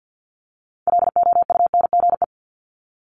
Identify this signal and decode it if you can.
Morse code, heard as a keyed tone, about 50 wpm, 700 Hz. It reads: LOUNGE
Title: Multi-letter Phonemes 50wpm